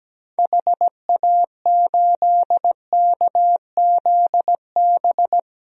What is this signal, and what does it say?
Morse code, heard as a keyed tone, about 17 wpm, 700 Hz. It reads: HA8KZB
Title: DXCC-100 Call Signs - 6 Characters - Repeated 3x - 17wpm